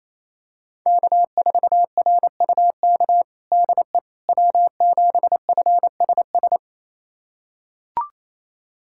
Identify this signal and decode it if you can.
Morse code, heard as a keyed tone, about 28 wpm, 700 Hz. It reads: K4RUK DE W7FSS
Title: USA Callsign DE Callsign - 5 Characters - 28wpm